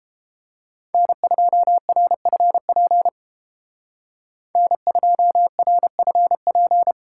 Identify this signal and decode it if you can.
Morse code, heard as a keyed tone, about 29 wpm, 700 Hz. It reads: N2RFP N2RFP